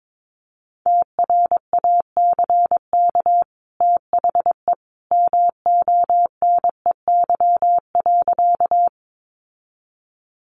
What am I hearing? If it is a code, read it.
Morse code, heard as a keyed tone, about 22 wpm, 700 Hz. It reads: TRACK THE MONEY.
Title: Sentences from Top 1100 Words 22wpm